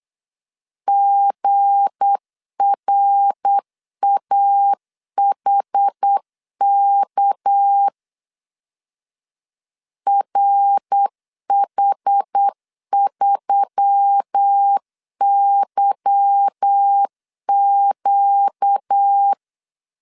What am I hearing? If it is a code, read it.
Morse code, heard as a keyed tone, about 8 wpm, 800 Hz. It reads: GRAHK RH3YQ
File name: LEZIONI / 8 YJ45 / 8 YJ4 8wpm 3246